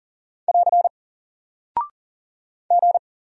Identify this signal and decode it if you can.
Morse code, heard as a keyed tone, about 39 wpm, 700 Hz. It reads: <AR> G
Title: Single Character (Letter:Number:Punctuation:Prosign) Alt 40wpm